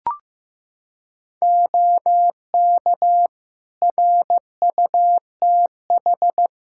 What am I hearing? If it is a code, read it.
Morse code, heard as a keyed tone, about 15 wpm, 700 Hz. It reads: OK RUTH
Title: Practice QSO - Exchange 2 - Info - 15wpm